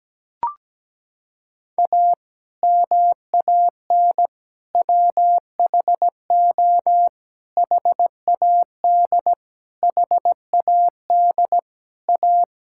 Morse code, heard as a keyed tone, about 17 wpm, 700 Hz. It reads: A MAN WHO HAD HAD A
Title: Sentences from Top 1000 Words - Review Part 6 - 17wpm